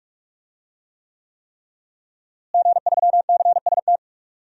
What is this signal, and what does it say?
Morse code, heard as a keyed tone, about 43 wpm, 700 Hz. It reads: G3CST
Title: DXCC-100 Call Signs - 5 Characters - Repeated 3x - 45wpm